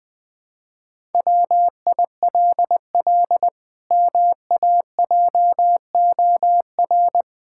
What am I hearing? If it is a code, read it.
Morse code, heard as a keyed tone, about 20 wpm, 700 Hz. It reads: WILL MAJOR
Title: Pangram Challenge v2 - 20wpm